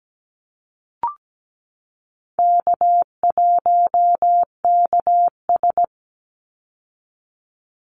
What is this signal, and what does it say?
Morse code, heard as a keyed tone, about 17 wpm, 700 Hz. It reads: K1KS
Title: Easy Call Signs Repeated 2x - 17wpm